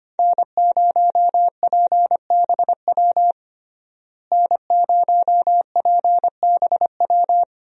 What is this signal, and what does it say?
Morse code, heard as a keyed tone, about 25 wpm, 700 Hz. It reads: N0PBW N0PBW
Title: USA Call Signs - 5 Characters - Repeated 2x - 25wpm